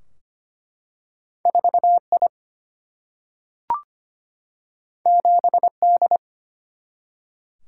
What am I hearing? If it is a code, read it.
Morse code, heard as a keyed tone, about 25 wpm, 700 Hz. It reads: 4I 7D
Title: Single-Letter Single-Number 25wpm